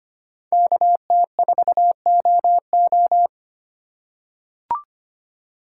Morse code, heard as a keyed tone, about 25 wpm, 700 Hz. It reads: KT4OO
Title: USA Call Signs - 5 Characters - 4x Character Spacing - 25wpm